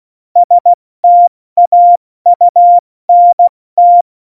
Morse code, heard as a keyed tone, about 16 wpm, 700 Hz. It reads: STAUNT
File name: Morsekurs / Witze / Witz 17. November 2020, 70 F80